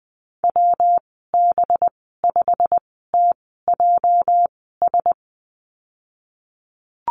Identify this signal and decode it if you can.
Morse code, heard as a keyed tone, about 20 wpm, spaced as for 15 wpm, 700 Hz. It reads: WB5TJS
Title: 231 Lesson 37g - Letter x - Call Signs 20wpm